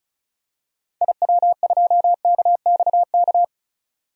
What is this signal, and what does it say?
Morse code, heard as a keyed tone, about 34 wpm, 700 Hz. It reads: IW2KXK
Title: DXCC-100 Call Signs - 6 Characters - 35wpm